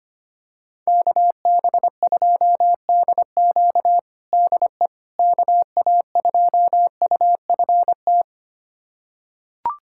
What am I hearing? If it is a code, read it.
Morse code, heard as a keyed tone, about 25 wpm, 700 Hz. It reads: KB2DQ DE KA2UFT
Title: USA Callsign DE Callsign 25wpm